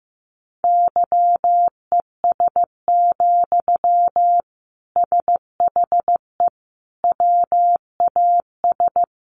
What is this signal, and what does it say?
Morse code, heard as a keyed tone, about 15 wpm, 700 Hz. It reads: YES, SHE WAS
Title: Sentences from Top 1100 Words 15wpm